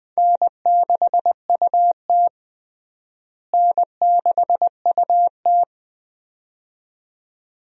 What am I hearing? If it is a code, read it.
Morse code, heard as a keyed tone, about 20 wpm, 700 Hz. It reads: N6UT N6UT
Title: USA Call Signs - 4 Characters - Repeated 2x - 20wpm